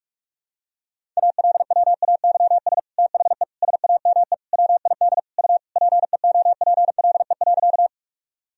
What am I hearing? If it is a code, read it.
Morse code, heard as a keyed tone, about 46 wpm, 700 Hz. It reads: ALWAYS THE SAME WID U PEOPLE.
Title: Sentences from Top 300 Words and Abbreviations 45wpm